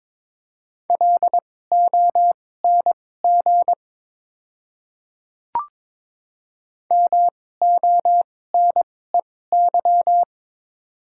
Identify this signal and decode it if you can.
Morse code, heard as a keyed tone, about 23 wpm, spaced as for 16 wpm, 700 Hz. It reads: LONG MONEY
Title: 2x Character Spacing - Top 500 Words - 22wpm